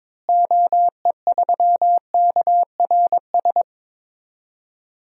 Morse code, heard as a keyed tone, about 22 wpm, 700 Hz. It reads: OE3KRS
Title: DXCC-100 Call Signs - 6 Characters - Repeated 2x - 22wpm